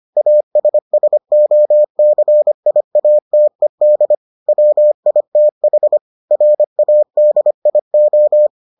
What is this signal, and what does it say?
Morse code, heard as a keyed tone, about 25 wpm, 600 Hz. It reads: ASSOCIATED WITH RADIO